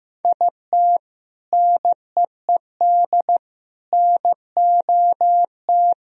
Morse code, heard as a keyed tone, about 15 wpm, 700 Hz. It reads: IT NEED NOT
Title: Sentences from Top 1000 Words - Review Part 4 - 15wpm